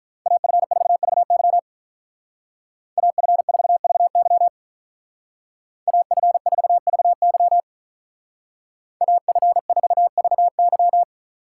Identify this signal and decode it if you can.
Morse code, heard as a keyed tone, about 42 wpm, 700 Hz. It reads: AF4VY AF4VY AF4VY AF4VY
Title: Speed Racing - USA 2x2 Call Signs 35wpm